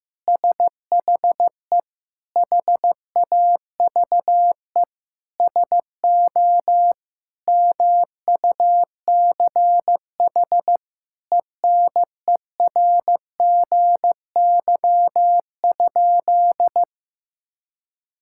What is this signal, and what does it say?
Morse code, heard as a keyed tone, about 15 wpm, 700 Hz. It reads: SHE HAVE SO MUCH ENERGY?
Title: Sentences from Top 1000 Words - Review Part 6 - 15wpm